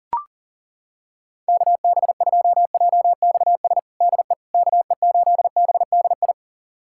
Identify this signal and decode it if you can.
Morse code, heard as a keyed tone, about 40 wpm, 700 Hz. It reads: KB2JXS DE KE8BDI
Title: USA Callsign DE Callsign 40wpm